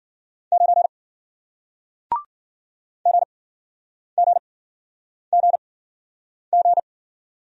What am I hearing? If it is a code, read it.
Morse code, heard as a keyed tone, about 58 wpm, spaced as for 9 wpm, 700 Hz. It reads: 9 GGGG